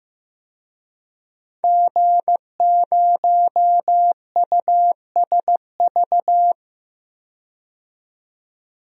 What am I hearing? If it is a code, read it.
Morse code, heard as a keyed tone, about 15 wpm, 700 Hz. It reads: G0USV
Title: DXCC-100 Call Signs - 5 Characters - 15wpm